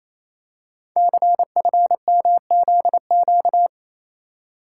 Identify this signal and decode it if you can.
Morse code, heard as a keyed tone, about 29 wpm, 700 Hz. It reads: CFMZQ
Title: Code Groups - 5 Letters - 28wpm